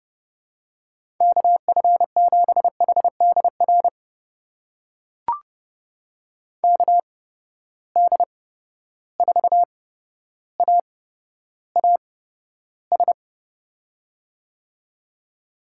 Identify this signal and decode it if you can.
Morse code, heard as a keyed tone, about 30 wpm, spaced as for 13 wpm, 700 Hz. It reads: KF7HDR KD4AAS
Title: USA Call Signs - 6 Characters - 8x Character Spacing - 30wpm